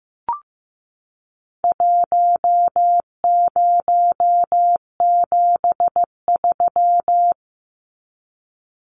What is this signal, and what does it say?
Morse code, heard as a keyed tone, about 15 wpm, 700 Hz. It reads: 1073
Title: Numbers - 4-Digit 15wpm